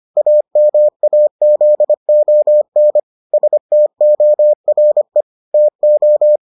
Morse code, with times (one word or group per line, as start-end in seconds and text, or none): 0.2-3.0 AMAZON
3.3-5.2 STORE
5.5-6.4 TO